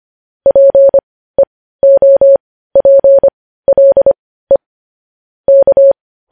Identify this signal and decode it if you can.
Morse code, heard as a keyed tone, about 25 wpm, spaced as for 15 wpm, 550 Hz. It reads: PEOPLE K
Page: news headlines in morse code at 15 wpm